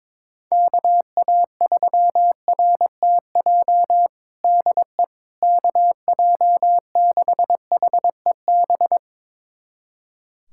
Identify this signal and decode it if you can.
Morse code, heard as a keyed tone, about 21 wpm, 700 Hz. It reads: KA3RTJ DE KJ6HEB
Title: USA Callsign DE Callsign 22wpm